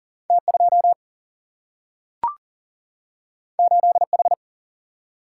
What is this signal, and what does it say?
Morse code, heard as a keyed tone, about 39 wpm, 700 Hz. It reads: T2 8H
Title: Single-Letter Single-Number 40wpm